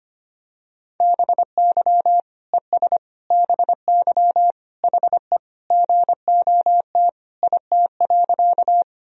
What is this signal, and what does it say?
Morse code, heard as a keyed tone, about 25 wpm, 700 Hz. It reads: BY ES BY HE GOT IT.
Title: Sentences from Top 300 Words and Abbreviations 25wpm